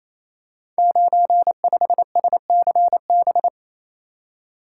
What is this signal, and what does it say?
Morse code, heard as a keyed tone, about 29 wpm, 690 Hz. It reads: 95SCB
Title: Code Groups - 5 Letter:Numbers - 28wpm